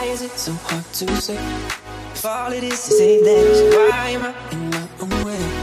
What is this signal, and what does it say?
Ringback tone (Europe, 425 Hz), with noise.